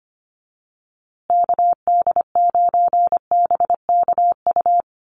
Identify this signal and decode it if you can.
Morse code, heard as a keyed tone, about 25 wpm, 700 Hz. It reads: KD9BKU
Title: USA Call Signs Repeated 2x 25wpm